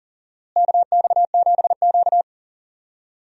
Morse code, heard as a keyed tone, about 40 wpm, 700 Hz. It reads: KX7Q
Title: USA Call Signs - 4 Characters - Repeated 3x - 40wpm